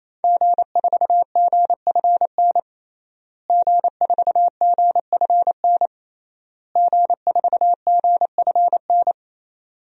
Morse code, heard as a keyed tone, about 27 wpm, 700 Hz. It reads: G4GFN G4GFN G4GFN
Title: DXCC-100 Call Signs - 5 Characters - Repeated 3x - 28wpm